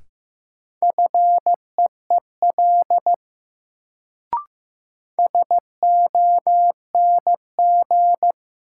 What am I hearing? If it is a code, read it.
Morse code, heard as a keyed tone, about 15 wpm, 700 Hz. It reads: FEEL SONG
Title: Sentences from Top 400 Words - New Word Drill - 15wpm